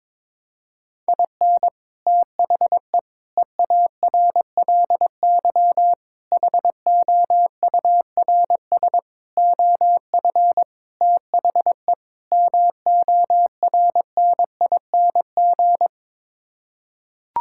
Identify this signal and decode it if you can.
Morse code, heard as a keyed tone, about 22 wpm, 700 Hz. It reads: IN THE EARLY HOURS OF THE MORNING
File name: practice / 22wpm / Sets of 7 Words - 22wpm